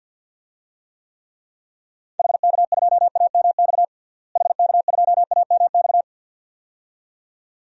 Morse code, heard as a keyed tone, about 52 wpm, 700 Hz. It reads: HK2AMX HK2AMX
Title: DXCC-100 Call Signs - 6 Characters - Repeated 2x - 50wpm